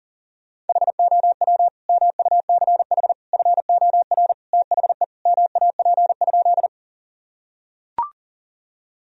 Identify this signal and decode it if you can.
Morse code, heard as a keyed tone, about 40 wpm, 700 Hz. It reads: HOW MUCH FOR THE MAP?
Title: Sentences from Top 1000 Words - Review Part 2 - 40wpm